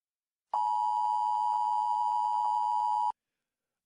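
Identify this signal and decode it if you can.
Emergency Alert System (SAME) tones (the 853 and 960 Hz attention tone). It is heard from 0.5 to 3.2 s.
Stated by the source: Emergency broadcast and ham radio signals in a fictional audio drama after a world-ending eruption.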